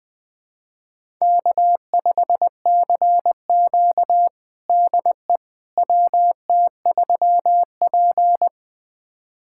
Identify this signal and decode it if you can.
Morse code, heard as a keyed tone, about 20 wpm, 700 Hz. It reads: K5CQ DE WT3P
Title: USA Callsign DE Callsign - 4 Characters - 20wpm